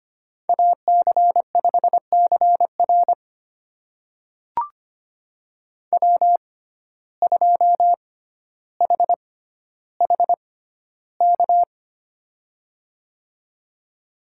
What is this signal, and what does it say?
Morse code, heard as a keyed tone, about 25 wpm, spaced as for 13 wpm, 700 Hz. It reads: AC5CR W2HHK
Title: USA Call Signs - 5 Characters - 6x Character Spacing - 25wpm